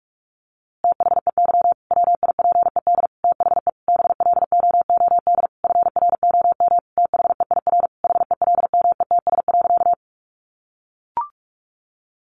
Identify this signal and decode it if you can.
Morse code, heard as a keyed tone, about 46 wpm, 700 Hz. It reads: THEY WIPED THE BLOOD FROM THEIR HELMETS.
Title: A Sound Of Thunder 45wpm